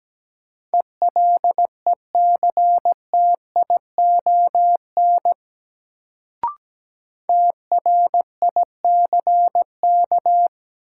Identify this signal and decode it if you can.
Morse code, heard as a keyed tone, about 17 wpm, 700 Hz. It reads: ELECTION TRICK
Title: Sentences from Top 1500 Words - New Word Drill Alt - 17wpm